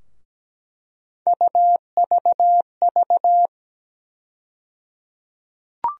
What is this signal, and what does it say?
Morse code, heard as a keyed tone, about 17 wpm, 700 Hz. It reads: UVV
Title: USA Call Sign Postfixes - 3 Letters 17wpm